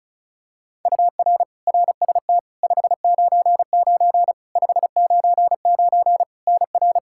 Morse code, heard as a keyed tone, about 36 wpm, 700 Hz. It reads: UR RST 599 599 NR
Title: K1USN SST QSO Practice 35wpm